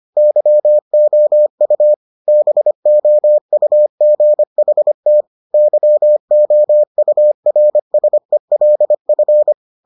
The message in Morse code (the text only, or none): YOU BOUGHT YOURSELF